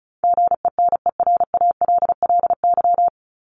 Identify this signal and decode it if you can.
Morse code, heard as a keyed tone, about 34 wpm, 700 Hz. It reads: GENERALLY
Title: LY Suffix - Top 100 Words with Base Words 35wpm